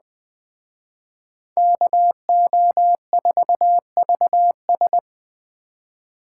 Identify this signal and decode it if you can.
Morse code, heard as a keyed tone, about 20 wpm, 700 Hz. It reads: KO4VS